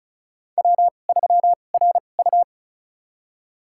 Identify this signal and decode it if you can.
Morse code, heard as a keyed tone, about 35 wpm, spaced as for 25 wpm, 700 Hz. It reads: W3RU
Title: USA Call Signs - 4 Characters - 2x Character Spacing - 35wpm